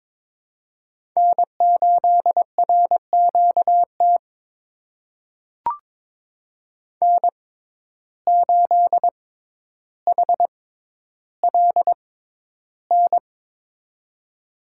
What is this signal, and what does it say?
Morse code, heard as a keyed tone, about 22 wpm, spaced as for 11 wpm, 700 Hz. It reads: N8RQT N8HLN
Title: USA Call Signs - 5 Characters - 6x Character Spacing - 22wpm